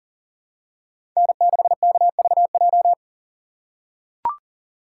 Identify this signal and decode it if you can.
Morse code, heard as a keyed tone, about 37 wpm, 700 Hz. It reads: N6KVJ
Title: USA Call Signs - 5 Characters - 3x Character Spacing - 40wpm